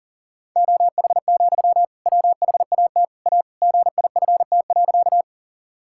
Morse code, heard as a keyed tone, about 37 wpm, 700 Hz. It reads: OH, WHAT A GIFT.
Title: Sentences from Top 1000 Words - Review Part 6 - 40wpm